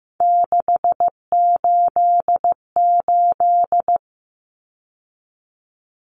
Morse code, heard as a keyed tone, about 15 wpm, 700 Hz. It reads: 688
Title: Numbers - 3-Digit 15wpm